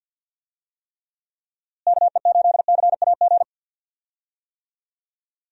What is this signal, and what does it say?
Morse code, heard as a keyed tone, about 50 wpm, 700 Hz. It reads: KE8CAG
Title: USA Call Signs - 6 Characters - Repeated 2x - 50wpm